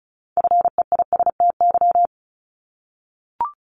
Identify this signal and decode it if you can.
Morse code, heard as a keyed tone, about 34 wpm, 700 Hz. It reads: FEISTY
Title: Multi-letter Phonemes Repeated 3x 35wpm